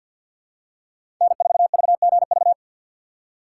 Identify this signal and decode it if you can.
Morse code, heard as a keyed tone, about 49 wpm, 700 Hz. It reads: N4VGV
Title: USA Call Signs - 5 Characters - 2x Character Spacing - 50wpm